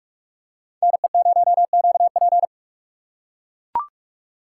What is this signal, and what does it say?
Morse code, heard as a keyed tone, about 43 wpm, 700 Hz. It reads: NE0QP
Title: USA Call Signs - 5 Characters - 3x Character Spacing - 45wpm